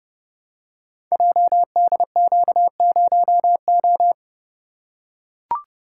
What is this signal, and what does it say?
Morse code, heard as a keyed tone, about 30 wpm, 700 Hz. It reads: JDQ0O
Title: Code Groups - 5 Letter:Numbers - 30wpm